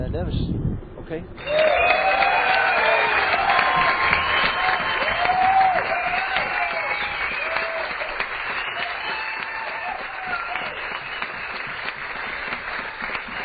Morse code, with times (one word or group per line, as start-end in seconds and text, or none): none